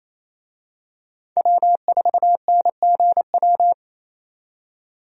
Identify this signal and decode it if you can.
Morse code, heard as a keyed tone, about 27 wpm, 700 Hz. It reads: W4NGW
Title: USA Call Signs - 5 Characters - 28wpm